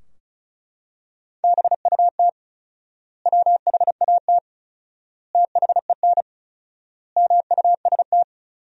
Morse code, heard as a keyed tone, about 36 wpm, 700 Hz. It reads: BUT WHAT THEN MUST
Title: 4x Word-Spacing - Sentences from Top 500 Words - 35wpm